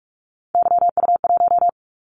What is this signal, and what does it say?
Morse code, heard as a keyed tone, about 46 wpm, 700 Hz. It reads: YU1
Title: Callsign Prefixes for Top 25 DXCC - 45wpm